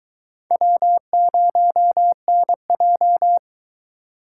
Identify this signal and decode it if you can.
Morse code, heard as a keyed tone, about 23 wpm, 700 Hz. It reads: W0NJ